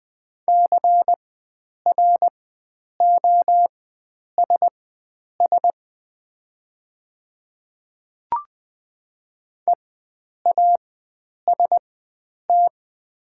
Morse code, heard as a keyed tone, about 20 wpm, spaced as for 9 wpm, 700 Hz. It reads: CROSS EAST
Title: 4x Character Spacing - Top 500 Words - 20wpm